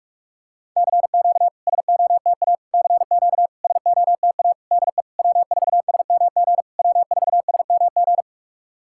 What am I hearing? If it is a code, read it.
Morse code, heard as a keyed tone, about 46 wpm, 690 Hz. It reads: CQ SOTA CQ SOTA DE W4SMG W4SMG
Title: SOTA QSO Practice 45wpm